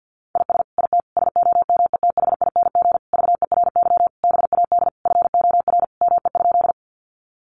Morse code, heard as a keyed tone, about 50 wpm, 700 Hz. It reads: IS IT SOMETHING VERY BAD FOR ME?
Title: Sentences from Top 700 Words 50wpm